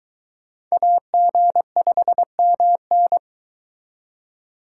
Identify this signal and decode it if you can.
Morse code, heard as a keyed tone, about 23 wpm, 700 Hz. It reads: AG5MN